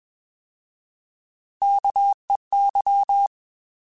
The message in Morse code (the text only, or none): KEY